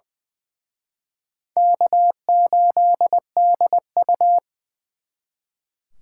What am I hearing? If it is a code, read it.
Morse code, heard as a keyed tone, about 23 wpm, 700 Hz. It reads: K8DU
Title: Speed Racing - USA 1x2 Call Signs 20wpm